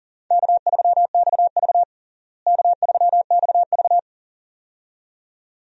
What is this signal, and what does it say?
Morse code, heard as a keyed tone, about 40 wpm, 700 Hz. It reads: K3XV K3XV
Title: USA Call Signs - 4 Characters - Repeated 2x - 40wpm